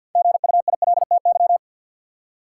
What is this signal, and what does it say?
Morse code, heard as a keyed tone, about 53 wpm, 700 Hz. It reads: GUILTY
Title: Six Letter Words - 50wpm